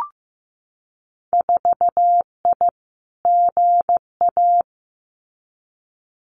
Morse code, heard as a keyed tone, about 15 wpm, 700 Hz. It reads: 4I GA
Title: Winter Field Day Exchanges Encore 15wpm